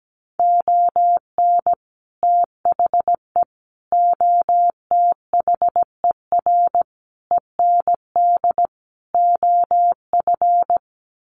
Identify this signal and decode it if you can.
Morse code, heard as a keyed tone, about 17 wpm, 700 Hz. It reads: ON THE OTHER END OF
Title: Sets of 5 Words Encore 17wpm